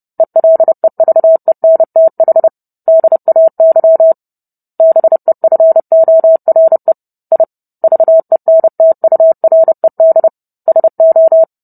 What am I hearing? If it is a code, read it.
Morse code, heard as a keyed tone, about 30 wpm, 650 Hz. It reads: ELEVENTH DAY BEFORE I VENTURED SO